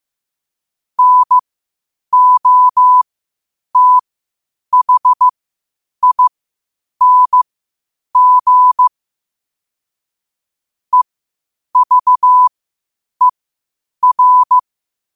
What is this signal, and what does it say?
Morse code, heard as a keyed tone, about 15 wpm, spaced as for 8 wpm, 1000 Hz. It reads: NOTHING EVER
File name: qotdfiles / 8wpm